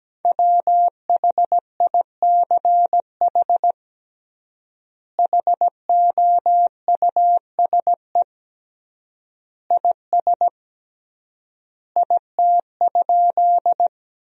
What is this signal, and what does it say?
Morse code, heard as a keyed tone, about 17 wpm, 700 Hz. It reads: WHICH HOUSE IS IT?
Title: 3x Word-Spacing - Sentences from Top 200 Words - 17wpm